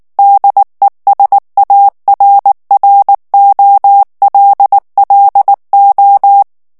Morse code, heard as a keyed tone, about 19 wpm, 800 Hz. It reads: DESARROLLO
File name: morse mp3 / ESPAGNOL TEXTE EN CLAIR 2 VITESSE 18 MOTS MINUTES